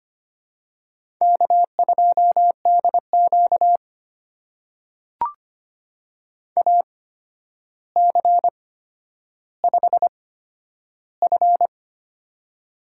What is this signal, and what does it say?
Morse code, heard as a keyed tone, about 25 wpm, spaced as for 11 wpm, 700 Hz. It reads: K2DQ AC5F
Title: USA Call Signs - 4 Characters - 8x Character Spacing - 25wpm